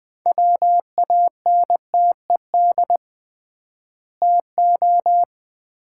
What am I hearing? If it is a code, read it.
Morse code, heard as a keyed tone, about 21 wpm, 700 Hz. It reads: WANTED TO
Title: 3x Word-Spacing - Sets of 2 Words - 20wpm